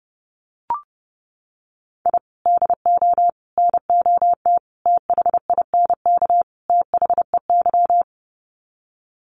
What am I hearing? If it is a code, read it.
Morse code, heard as a keyed tone, about 30 wpm, 700 Hz. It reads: I DO NOT THINK THEY
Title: Sets of 5 Words 30wpm